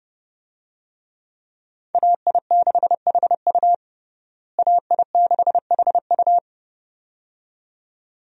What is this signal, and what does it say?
Morse code, heard as a keyed tone, about 30 wpm, 700 Hz. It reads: AI6HU AI6HU
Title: USA Call Signs - 5 Characters - Repeated 2x - 30wpm